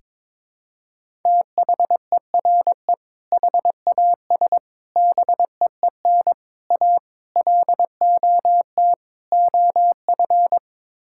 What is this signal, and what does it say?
Morse code, heard as a keyed tone, about 21 wpm, 700 Hz. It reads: THERE HAS BEEN A LOT OF
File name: practice / 22wpm / Sets of 6 Words - 22wpm